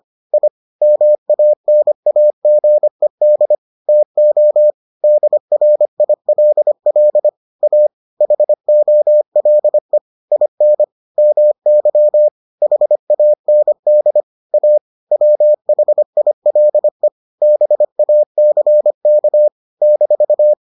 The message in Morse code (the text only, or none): I MANAGED TO DRILL A HOLE IN MY HAND A WHILE BACK -